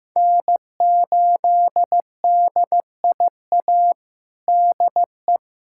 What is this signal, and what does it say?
Morse code, heard as a keyed tone, about 15 wpm, 700 Hz. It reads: N8DIA DE
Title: USA Callsign DE Callsign - 5 Characters - 15wpm